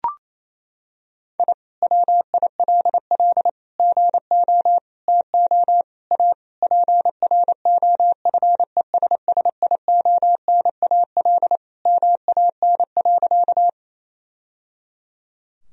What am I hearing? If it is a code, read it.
Morse code, heard as a keyed tone, about 28 wpm, 700 Hz. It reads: I WILL GO TO A PROFESSIONAL MAN.
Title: Sentences from Top 1000 Words - Review Part 4 - 28wpm